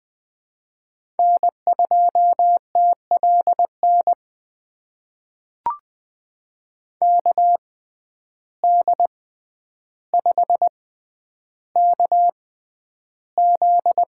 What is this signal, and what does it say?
Morse code, heard as a keyed tone, about 20 wpm, spaced as for 10 wpm, 700 Hz. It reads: N2TLN KD5KZ
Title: USA Call Signs - 6x Character Spacing - 20wpm